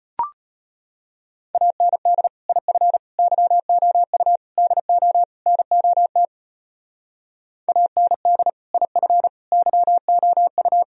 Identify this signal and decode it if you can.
Morse code, heard as a keyed tone, about 32 wpm, 700 Hz. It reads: AND IF YOU DO NOT AND IF YOU